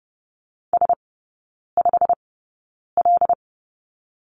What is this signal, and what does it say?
Morse code, heard as a keyed tone, about 30 wpm, spaced as for 9 wpm, 700 Hz. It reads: S5L
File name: course / 30:10wpm / 087 Lesson 15e - Number 5 - Three Characters 30wpm